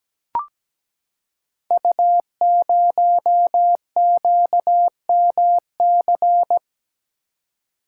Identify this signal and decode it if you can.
Morse code, heard as a keyed tone, about 17 wpm, 700 Hz. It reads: U0QMC
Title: Code Groups - 5 Letter:Numbers - 17wpm